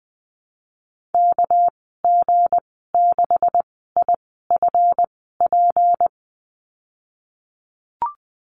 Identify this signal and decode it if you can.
Morse code, heard as a keyed tone, about 20 wpm, spaced as for 14 wpm, 700 Hz. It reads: KG6IFP
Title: 245 Lesson 39g - Letter z - Call Signs 20wpm